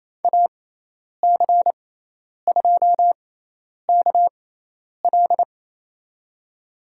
Morse code, heard as a keyed tone, about 28 wpm, spaced as for 10 wpm, 700 Hz. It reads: AC2KL
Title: USA Call Signs - 5 Characters - 6x Character Spacing - 28wpm